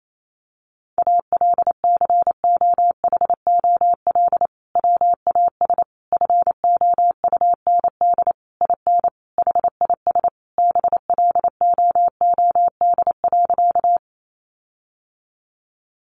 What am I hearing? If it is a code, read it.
Morse code, heard as a keyed tone, about 28 wpm, 700 Hz. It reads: ALCOHOL WAS FOUND IN HIS BLOOD.
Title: Sentences from Top 1400 Words 28wpm